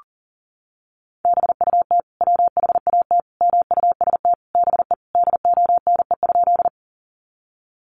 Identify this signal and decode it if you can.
Morse code, heard as a keyed tone, about 40 wpm, 700 Hz. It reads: BUT WHAT MUST BE DONE?
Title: Sentences from Top 500 Words 40wpm